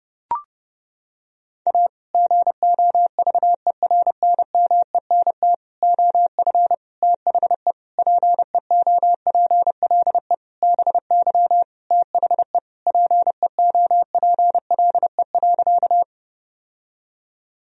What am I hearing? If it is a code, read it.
Morse code, heard as a keyed tone, about 30 wpm, 700 Hz. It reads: A GOVERNMENT OF THE PEOPLE BY THE PEOPLE.
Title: Sentences from Top 1000 Words - Review Part 3 - 30wpm